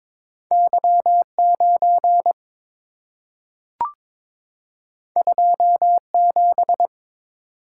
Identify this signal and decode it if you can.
Morse code, heard as a keyed tone, about 22 wpm, 700 Hz. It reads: Y9 27
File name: practice / 22wpm / Two Letter-Number Combinations - 22wpm